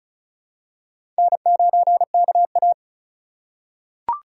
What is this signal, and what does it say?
Morse code, heard as a keyed tone, about 36 wpm, 700 Hz. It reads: N9KA
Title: USA Call Signs - 4 Characters - 2x Character Spacing - 35wpm